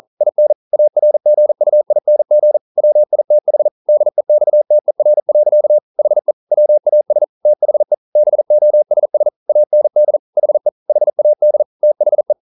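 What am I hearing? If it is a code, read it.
Morse code, heard as a keyed tone, about 43 wpm, 600 Hz. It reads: IN ARGUING WITH DEXTER. HE WAS THE BOSS AND HE HAD THE
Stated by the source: Morse code practice at 41 wpm, from CW med Gnister.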